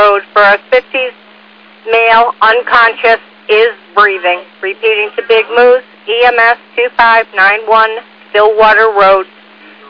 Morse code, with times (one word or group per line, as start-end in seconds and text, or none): none